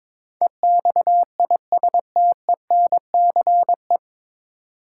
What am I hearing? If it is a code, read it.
Morse code, heard as a keyed tone, about 22 wpm, 700 Hz. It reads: EXISTENCE
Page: Morse Code Ninja practice 22wpm